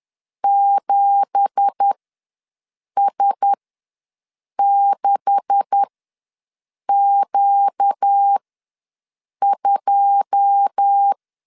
Morse code, heard as a keyed tone, about 10 wpm, spaced as for 6 wpm, 800 Hz. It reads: 7S6Q2
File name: LEZIONI / 9 2867 ws 10wpm 4215